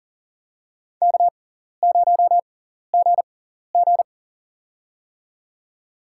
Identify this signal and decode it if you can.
Morse code, heard as a keyed tone, about 40 wpm, spaced as for 14 wpm, 700 Hz. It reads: K0GG
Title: USA Call Signs - 4 Characters - 6x Character Spacing - 40wpm